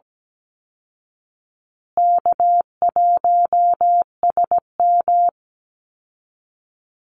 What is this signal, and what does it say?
Morse code, heard as a keyed tone, about 17 wpm, 700 Hz. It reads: K1SM
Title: Call Signs Repeated 3x - 17wpm